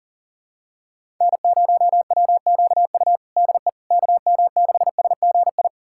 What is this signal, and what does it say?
Morse code, heard as a keyed tone, about 40 wpm, 700 Hz. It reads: N0WQU DE KM6SGI
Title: USA Callsign DE Callsign 40wpm